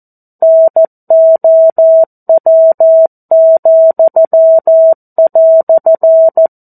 Morse code, heard as a keyed tone, about 14 wpm, 650 Hz. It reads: NOW,"
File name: 02-Chapter-02 14 WPM